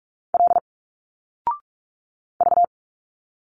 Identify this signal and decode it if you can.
Morse code, heard as a keyed tone, about 45 wpm, 700 Hz. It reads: L V
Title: Single Letters 45wpm